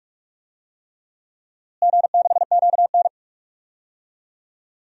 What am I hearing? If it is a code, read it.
Morse code, heard as a keyed tone, about 42 wpm, 700 Hz. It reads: G6QN
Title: DXCC-100 Call Signs - 4 Characters - Repeated 3x - 45wpm